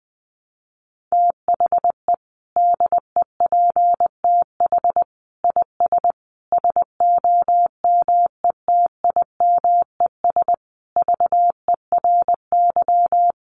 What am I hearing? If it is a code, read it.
Morse code, heard as a keyed tone, about 20 wpm, 700 Hz. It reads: THE DEPTH IS SOMETIMES VERY
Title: Sentences from Top 1200 Words 20wpm